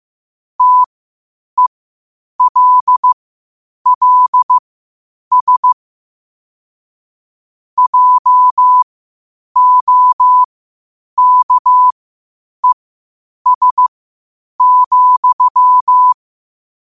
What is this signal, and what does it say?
Morse code, heard as a keyed tone, about 15 wpm, spaced as for 8 wpm, 1000 Hz. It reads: TELLS JOKES,